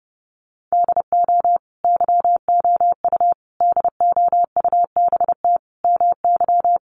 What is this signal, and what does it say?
Morse code, heard as a keyed tone, about 30 wpm, 700 Hz. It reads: DO YOU DOUBT MY
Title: Sentences from Top 1500 Words 30wpm